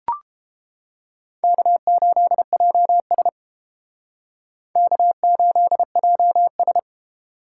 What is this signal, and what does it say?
Morse code, heard as a keyed tone, about 33 wpm, 700 Hz. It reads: K8JS K8JS